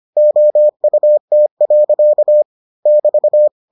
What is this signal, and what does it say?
Morse code, heard as a keyed tone, about 25 wpm, 600 Hz. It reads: OUT. <BT>